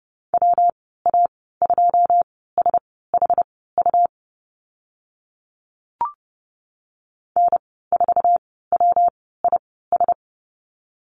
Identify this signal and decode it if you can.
Morse code, heard as a keyed tone, about 30 wpm, spaced as for 17 wpm, 700 Hz. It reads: WA2SHU N4WIS
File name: Lesson 19g - Letter w - Call Signs 30wpm